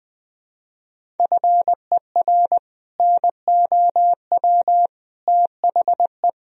Morse code, heard as a keyed tone, about 20 wpm, 700 Hz. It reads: FER NOW THE
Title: Sentences from Top 200 Words and Abbreviations 20wpm